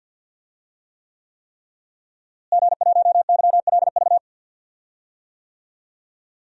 Morse code, heard as a keyed tone, about 49 wpm, 700 Hz. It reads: G1YLV
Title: DXCC-100 Call Signs - 5 Characters - Repeated 3x - 50wpm